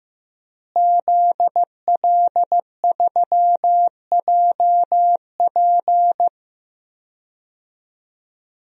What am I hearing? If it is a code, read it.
Morse code, heard as a keyed tone, about 15 wpm, 700 Hz. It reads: ZL3JP
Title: DXCC-100 Call Signs - 5 Characters - Repeated 2x - 15wpm